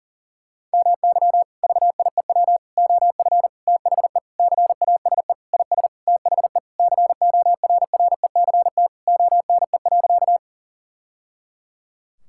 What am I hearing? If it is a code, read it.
Morse code, heard as a keyed tone, about 40 wpm, 700 Hz. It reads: MY VIEW OF THE CASE IS THE CORRECT ONE.
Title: Sentences from Top 1000 Words - Review Part 6 - 40wpm